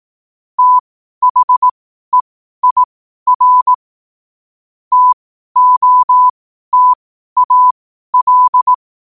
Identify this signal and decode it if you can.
Morse code, heard as a keyed tone, about 18 wpm, spaced as for 12 wpm, 1000 Hz. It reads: THEIR TOTAL
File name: qotdfiles / 12wpm